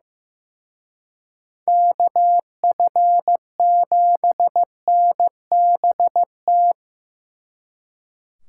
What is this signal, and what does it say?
Morse code, heard as a keyed tone, about 18 wpm, 700 Hz. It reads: KF7NBT
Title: Speed Racing - USA 2x3 Call Signs 15wpm